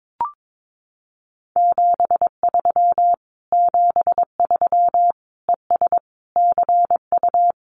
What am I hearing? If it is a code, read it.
Morse code, heard as a keyed tone, about 22 wpm, 700 Hz. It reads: 73 73 ES CU
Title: Common QSO Phrases Encore 22wpm